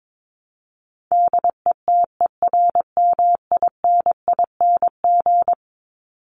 Morse code, heard as a keyed tone, about 22 wpm, 700 Hz. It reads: DETERMINING
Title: ING Suffix - Top 100 Words with Base Words 22wpm